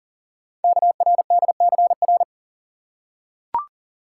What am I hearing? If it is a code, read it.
Morse code, heard as a keyed tone, about 42 wpm, 700 Hz. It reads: KRDCR
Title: Code Groups - 5 Letters - 40wpm